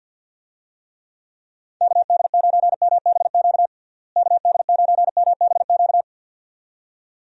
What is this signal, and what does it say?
Morse code, heard as a keyed tone, about 52 wpm, 700 Hz. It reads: KD9MBQ KD9MBQ
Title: USA Call Signs - 6 Characters - Repeated 2x - 50wpm